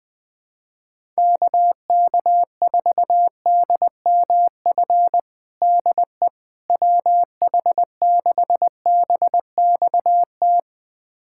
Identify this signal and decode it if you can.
Morse code, heard as a keyed tone, about 21 wpm, 700 Hz. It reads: KK4DMF DE WH6BXT
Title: USA Callsign DE Callsign 20wpm